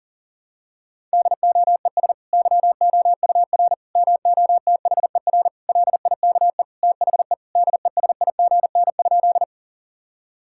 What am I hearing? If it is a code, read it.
Morse code, heard as a keyed tone, about 40 wpm, 700 Hz. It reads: DOES YOUR MOTHER LIKE THE DESIGN?
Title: Sentences from Top 1000 Words - Review Part 5 - 40wpm